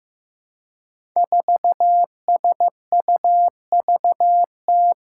Morse code, heard as a keyed tone, about 15 wpm, 700 Hz. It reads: 4SUVT
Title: Five Challenging Letter-Number Combinations 15wpm